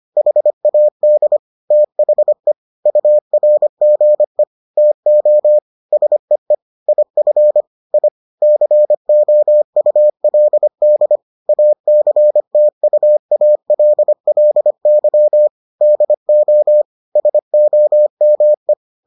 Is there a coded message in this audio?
Yes: Morse code from 0.2 to 18.7 s, heard as a keyed tone, about 25 wpm, 600 Hz. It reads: HAD THE URGE TO SEE IF I COULD ACTUALLY DO SOME